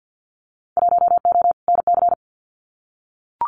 Morse code, heard as a keyed tone, about 53 wpm, 700 Hz. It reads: 1O NC
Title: Winter Field Day Exchanges 50wpm